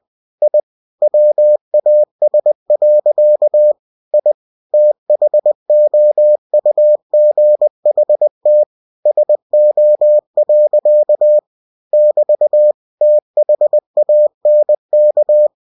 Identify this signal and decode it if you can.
Morse code, heard as a keyed tone, about 20 wpm, 600 Hz. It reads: I WAS. I THOUGHT SO. <BT> THANK
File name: cw-books / winnie-the-pooh / winnie-20wpm-page-0084